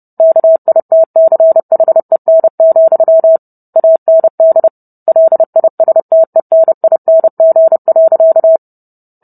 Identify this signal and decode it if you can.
Morse code, heard as a keyed tone, about 30 wpm, 650 Hz. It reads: KITCHEN, AND LISTENING.